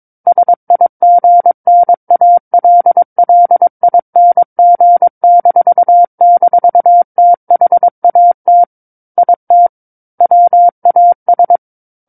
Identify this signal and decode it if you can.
Morse code, heard as a keyed tone, about 22 wpm, 700 Hz. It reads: SIGNALLING--THAT IT WAS
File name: Book One / 13-Chapter 13 - 22 WPM